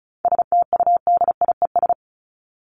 Morse code, heard as a keyed tone, about 34 wpm, 700 Hz. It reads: STUDIES
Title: ES:S Suffix - Top 100 Words with Base Words 35wpm